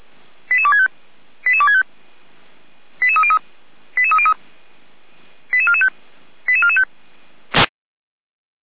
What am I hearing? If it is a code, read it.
Five-tone selective call (ZVEI), heard as quick runs of five tones, tones of about 70 ms. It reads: 88265 88265 88303 88303 88445 88445
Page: /455/SDR/ZVEI/